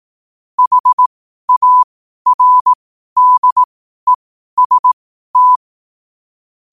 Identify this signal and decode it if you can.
Morse code, heard as a keyed tone, about 18 wpm, spaced as for 12 wpm, 1000 Hz. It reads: HARDEST
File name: qotdfiles / 12wpm